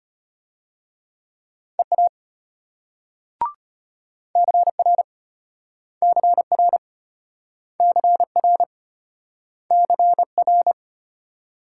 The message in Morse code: EA CR CR CR CR

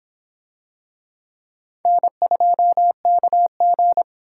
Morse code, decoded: N2KG

